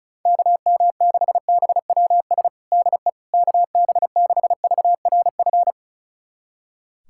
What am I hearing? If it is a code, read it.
Morse code, heard as a keyed tone, about 36 wpm, 700 Hz. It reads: KM6BWS DE KB6VRF
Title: USA Callsign DE Callsign - 6 Characters - 35wpm